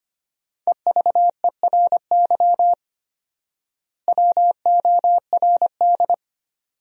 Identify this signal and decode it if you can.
Morse code, heard as a keyed tone, about 25 wpm, 700 Hz. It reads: EVERY WORD
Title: 4x Word-Spacing - Sentences from Top 400 Words - 25wpm